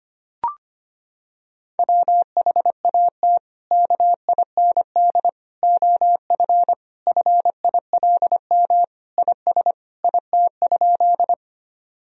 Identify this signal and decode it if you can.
Morse code, heard as a keyed tone, about 25 wpm, 700 Hz. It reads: WHAT KIND OF FILM IS IT?
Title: Sentences from Top 1000 Words - Review Part 3 - 25wpm